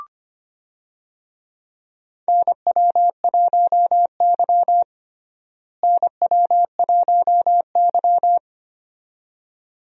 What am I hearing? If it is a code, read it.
Morse code, heard as a keyed tone, about 25 wpm, 700 Hz. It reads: NW1Y NW1Y
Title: USA Call Signs - 4 Characters - Repeated 2x - 25wpm